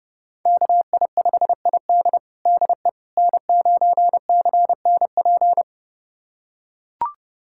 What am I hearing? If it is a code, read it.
Morse code, heard as a keyed tone, about 32 wpm, 700 Hz. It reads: KI5ID DE N9CNP
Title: USA Callsign DE Callsign - 5 Characters - 30wpm